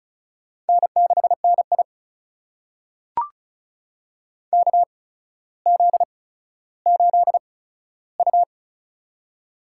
Morse code, heard as a keyed tone, about 34 wpm, spaced as for 15 wpm, 700 Hz. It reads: N6NI KZ8U
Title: USA Call Signs - 4 Characters - 8x Character Spacing - 35wpm